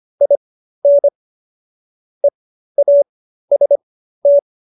Morse code, heard as a keyed tone, about 25 wpm, spaced as for 13 wpm, 570 Hz. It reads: IN EAST